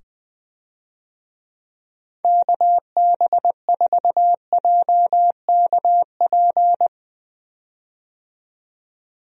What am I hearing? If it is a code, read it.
Morse code, heard as a keyed tone, about 20 wpm, 700 Hz. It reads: KB4JKP
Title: USA Call Signs - 6 Characters - 20wpm